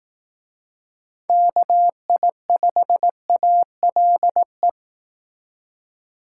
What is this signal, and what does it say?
Morse code, heard as a keyed tone, about 18 wpm, 700 Hz. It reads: KI5ALE
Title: Speed Racing - USA 2x3 Call Signs 15wpm